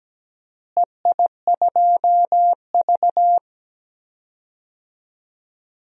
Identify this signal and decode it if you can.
Morse code, heard as a keyed tone, about 17 wpm, 700 Hz. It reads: EI2V
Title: DXCC-100 Call Signs - 4 Characters - Repeated 2x - 17wpm